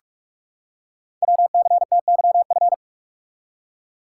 Morse code, heard as a keyed tone, about 44 wpm, 700 Hz. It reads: WCTYF
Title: Code Groups - 5 Letter:Numbers - 45wpm